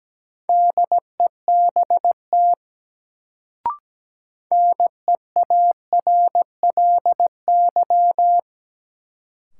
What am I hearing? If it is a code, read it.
Morse code, heard as a keyed tone, about 17 wpm, 700 Hz. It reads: DEBT NEARLY
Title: Sentences from Top 1300 Words - New Word Drill - 17wpm